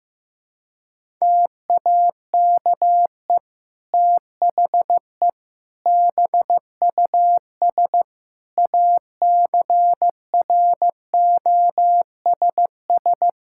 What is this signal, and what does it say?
Morse code, heard as a keyed tone, about 16 wpm, 700 Hz. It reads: TAKE THE BUS ACROSS
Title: Sentences from Top 1000 Words - Review Part 6 - 15wpm